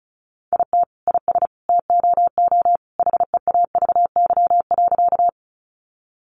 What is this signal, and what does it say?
Morse code, heard as a keyed tone, about 34 wpm, 700 Hz. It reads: IT IS TOO HEAVY.
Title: Sentences from Top 1100 Words 35wpm